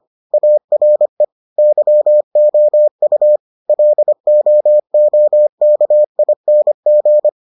Morse code, heard as a keyed tone, about 25 wpm, 600 Hz. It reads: ARE YOU LOOKING